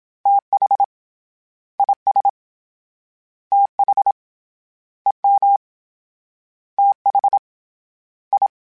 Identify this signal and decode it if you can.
Morse code, heard as a keyed tone, about 27 wpm, 800 Hz. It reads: TH IS TH EM TH I